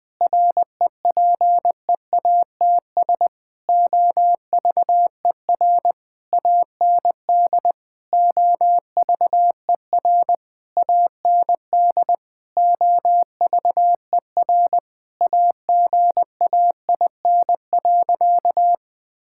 Morse code, heard as a keyed tone, about 20 wpm, 700 Hz. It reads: REPEATS OVER AND OVER AND OVER AGAIN.
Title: Sentences with 6-Word Phrases - Spoken Beginning - 20wpm